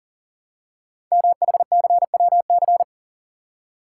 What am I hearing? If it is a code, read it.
Morse code, heard as a keyed tone, about 42 wpm, 700 Hz. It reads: MHCWC
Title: Code Groups - 5 Letter:Numbers - 40wpm